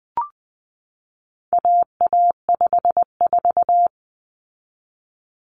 Morse code, heard as a keyed tone, about 20 wpm, 700 Hz. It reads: AA54
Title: Common Cut Numbers - 4-Digit 20wpm